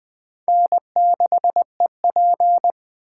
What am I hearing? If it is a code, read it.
Morse code, heard as a keyed tone, about 20 wpm, 700 Hz. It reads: N6EP
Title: USA Call Signs - 4 Characters - 2x Character Spacing - 20wpm